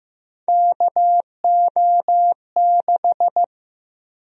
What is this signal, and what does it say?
Morse code, heard as a keyed tone, about 15 wpm, 700 Hz. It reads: KO6
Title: USA Call Sign Prefixes - Repeated 2x - 15wpm